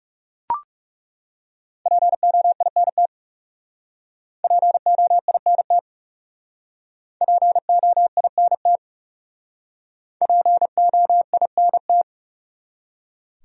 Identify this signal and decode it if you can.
Morse code, heard as a keyed tone, about 37 wpm, 700 Hz. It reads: POINT POINT POINT POINT
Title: Speed-Racing Top 300 Words 30wpm